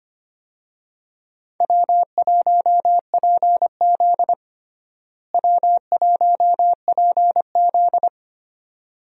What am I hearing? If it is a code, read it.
Morse code, heard as a keyed tone, about 25 wpm, 700 Hz. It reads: W1PZ W1PZ
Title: USA Call Signs - 4 Characters - Repeated 2x - 25wpm